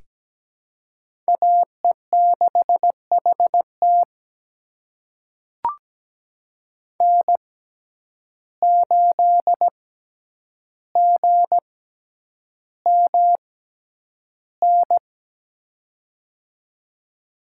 Morse code, heard as a keyed tone, about 17 wpm, spaced as for 9 wpm, 700 Hz. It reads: AE6HT N8GMN
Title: USA Call Signs - 5 Characters - 6x Character Spacing - 17wpm